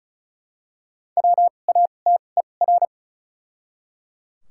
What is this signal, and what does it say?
Morse code, heard as a keyed tone, about 36 wpm, spaced as for 25 wpm, 700 Hz. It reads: WATER
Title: 2x Character Spacing - Top 100 Words - 35wpm